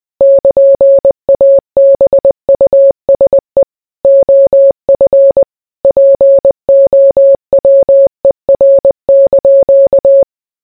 Morse code, heard as a keyed tone, about 20 wpm, 550 Hz. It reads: <KN>ABUSE OF POWER)